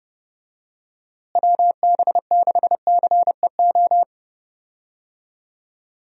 Morse code, heard as a keyed tone, about 31 wpm, 700 Hz. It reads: WB6CEO
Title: USA Call Signs - 6 Characters - Repeated 3x - 30wpm